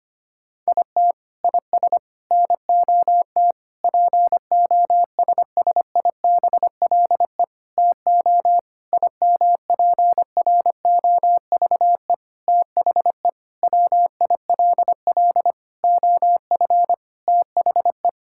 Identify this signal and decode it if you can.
Morse code, heard as a keyed tone, about 25 wpm, 700 Hz. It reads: IT IS NOT POSSIBLE TO IMPROVE THE WILL OF THE
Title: Sentences from Top 1000 Words - Review Part 5 - 25wpm